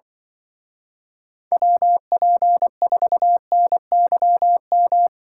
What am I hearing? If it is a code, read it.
Morse code, heard as a keyed tone, about 24 wpm, 700 Hz. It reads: WP4NYM
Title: Speed Racing - USA Call Signs 20wpm